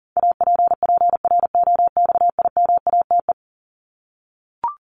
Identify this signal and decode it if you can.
Morse code, heard as a keyed tone, about 37 wpm, 700 Hz. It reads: APPROXIMATE
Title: LY Suffix - Top 100 Words with Base Words 40wpm